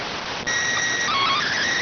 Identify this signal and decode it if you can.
SSTV (slow-scan television), PD 290.